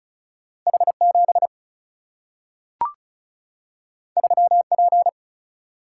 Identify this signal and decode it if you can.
Morse code, heard as a keyed tone, about 36 wpm, 700 Hz. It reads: H 7 3 P
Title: Single-Letter Single-Number 35wpm